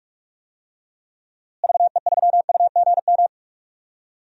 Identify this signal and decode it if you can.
Morse code, heard as a keyed tone, about 43 wpm, 700 Hz. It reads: VE3UGM
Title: DXCC-100 Call Signs - 6 Characters - Repeated 2x - 45wpm